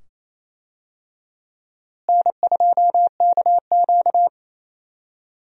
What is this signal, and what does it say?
Morse code, heard as a keyed tone, about 28 wpm, 700 Hz. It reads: N2KQ
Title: USA Call Signs - 4 Characters - Repeated 2x - 28wpm